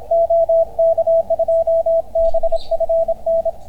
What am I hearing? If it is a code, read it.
Morse code, heard as a keyed tone, about 31 wpm, 660 Hz, with radio noise. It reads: OK2BFN